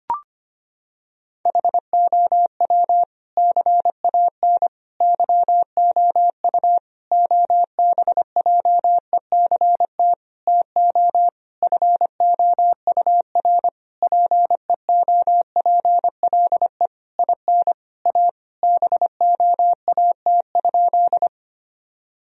Morse code, heard as a keyed tone, about 25 wpm, 700 Hz. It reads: HOW CAN YOU OBJECT TO FOUR PEOPLE IN A BOAT?
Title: Sentences from Top 1000 Words - Review Part 3 - 25wpm